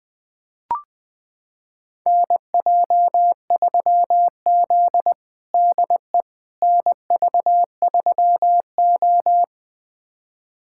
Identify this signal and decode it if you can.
Morse code, heard as a keyed tone, about 20 wpm, 700 Hz. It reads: NJ3Z DE NV3O
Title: USA Callsign DE Callsign - 4 Characters - 20wpm